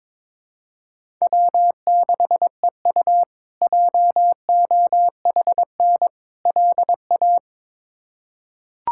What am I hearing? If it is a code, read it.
Morse code, heard as a keyed tone, about 22 wpm, 700 Hz. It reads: W6EU JOHN LA
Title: K1USN SST QSO Practice - 22wpm